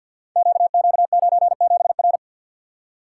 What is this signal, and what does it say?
Morse code, heard as a keyed tone, about 47 wpm, 700 Hz. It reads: QQ97R
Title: Code Groups - 5 Letter:Numbers - 50wpm